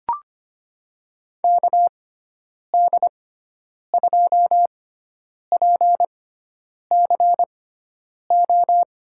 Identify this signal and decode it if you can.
Morse code, heard as a keyed tone, about 25 wpm, spaced as for 9 wpm, 700 Hz. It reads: KD2PCO
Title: USA Call Signs - 6 Characters - 6x Character Spacing - 25wpm